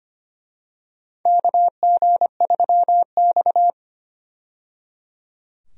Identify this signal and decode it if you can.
Morse code, heard as a keyed tone, about 25 wpm, 700 Hz. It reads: KG3X
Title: USA Call Signs - 4 Characters - 25wpm